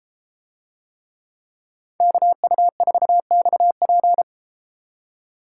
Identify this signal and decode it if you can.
Morse code, heard as a keyed tone, about 32 wpm, 700 Hz. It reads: KU4XP